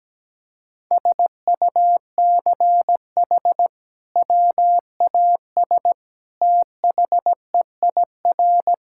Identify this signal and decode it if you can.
Morse code, heard as a keyed tone, about 17 wpm, 700 Hz. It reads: SUCH WAS THEIR
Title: Sentences from Top 1000 Words - Review Part 5 - 17wpm